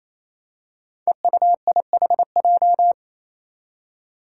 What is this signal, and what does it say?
Morse code, heard as a keyed tone, about 28 wpm, 700 Hz. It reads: EUIHJ
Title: Five Challenging Letter-Number Combinations 28wpm